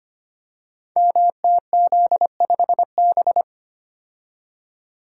Morse code, heard as a keyed tone, about 25 wpm, 700 Hz. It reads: MTZ5B
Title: Code Groups - 5 Letter:Numbers - 25wpm